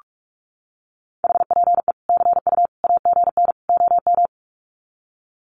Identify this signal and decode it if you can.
Morse code, heard as a keyed tone, about 44 wpm, 700 Hz. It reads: HPE CU AGN OM
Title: Common QSO Phrases 45wpm